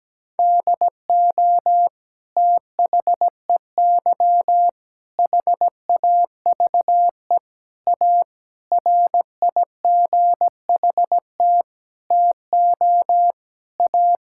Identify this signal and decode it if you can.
Morse code, heard as a keyed tone, about 17 wpm, 700 Hz. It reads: DO THEY HAVE A RIGHT TO A
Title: Sentences from Top 1000 Words - Review Part 6 - 17wpm